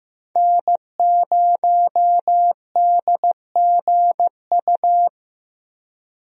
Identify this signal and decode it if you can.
Morse code, heard as a keyed tone, about 15 wpm, 700 Hz. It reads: N0DGU
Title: USA Call Signs - 5 Characters - Repeated 3x - 15wpm